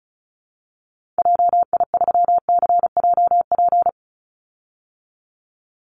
Morse code, heard as a keyed tone, about 34 wpm, 700 Hz. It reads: JI3CJP
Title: Call Signs Repeated 3x - 35wpm